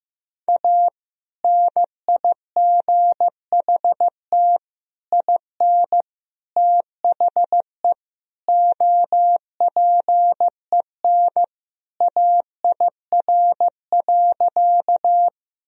Morse code, heard as a keyed tone, about 15 wpm, 700 Hz. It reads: A NIGHT IN THE OPEN AIR.
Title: Sentences from Top 1000 Words - Review Part 1 - 15wpm